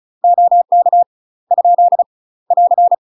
Morse code, heard as a keyed tone, about 33 wpm, 700 Hz. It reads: OK ? <AR>
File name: UK MASTER QSOs TRAINING / 35WPM / UK MASTER QSOs 0012